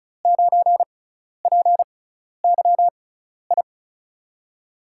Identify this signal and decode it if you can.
Morse code, heard as a keyed tone, about 36 wpm, spaced as for 12 wpm, 700 Hz. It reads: 9PYI
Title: USA Call Signs - 6 Characters - 6x Character Spacing - 35wpm